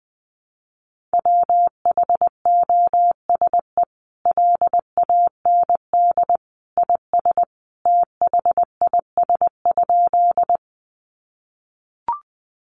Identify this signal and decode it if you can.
Morse code, heard as a keyed tone, about 20 wpm, 700 Hz. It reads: WHOSE LAND IS THIS?
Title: Sentences from Top 1200 Words 20wpm